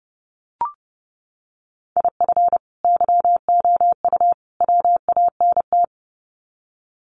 Morse code, heard as a keyed tone, about 29 wpm, 700 Hz. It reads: IF YOU WANT
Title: Sets of 3 Words 30wpm